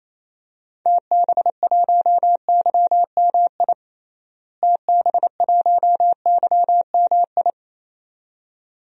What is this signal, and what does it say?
Morse code, heard as a keyed tone, about 28 wpm, 700 Hz. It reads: TB1YMI TB1YMI
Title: DXCC-100 Call Signs - 6 Characters - Repeated 2x - 28wpm